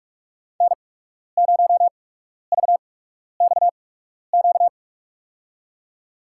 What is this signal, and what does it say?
Morse code, heard as a keyed tone, about 47 wpm, spaced as for 12 wpm, 700 Hz. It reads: N0VXQ
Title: USA Call Signs - 5 Characters - 8x Character Spacing - 45wpm